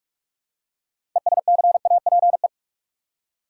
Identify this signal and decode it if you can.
Morse code, heard as a keyed tone, about 47 wpm, 700 Hz. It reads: ESCAPE
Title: Six Letter Words - 45wpm